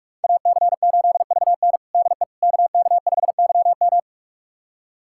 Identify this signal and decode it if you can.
Morse code, heard as a keyed tone, about 46 wpm, 700 Hz. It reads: AC8VN DE KK5YM
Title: USA Callsign DE Callsign - 5 Characters - 45wpm